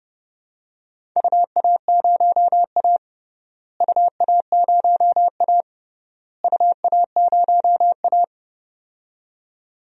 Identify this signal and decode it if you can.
Morse code, heard as a keyed tone, about 32 wpm, 700 Hz. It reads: UA0A UA0A UA0A
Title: DXCC-100 Call Signs - 4 Characters - Repeated 3x - 30wpm